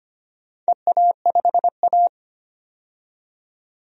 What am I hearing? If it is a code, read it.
Morse code, heard as a keyed tone, about 25 wpm, 700 Hz. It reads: EA5A
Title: DXCC-100 Call Signs - 4 Characters - 25wpm